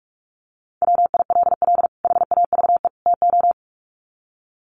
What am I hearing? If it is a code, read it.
Morse code, heard as a keyed tone, about 44 wpm, 700 Hz. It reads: WILL HAVE TO
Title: Sets of 3 Words 45wpm